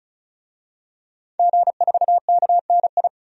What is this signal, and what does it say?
Morse code, heard as a keyed tone, about 36 wpm, 700 Hz. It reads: G4KNI
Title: DXCC-100 Call Signs - 5 Characters - 35wpm